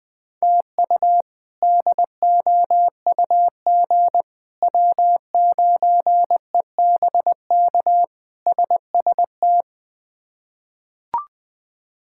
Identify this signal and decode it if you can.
Morse code, heard as a keyed tone, about 20 wpm, 700 Hz. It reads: TU DOUG W9EBK SST
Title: K1USN SST QSO Practice - 20wpm